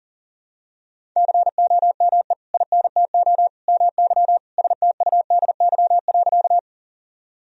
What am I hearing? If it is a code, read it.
Morse code, heard as a keyed tone, about 40 wpm, 700 Hz. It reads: COME INTO MY STUDY.
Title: Sentences from Top 1000 Words - Review Part 1 - 40wpm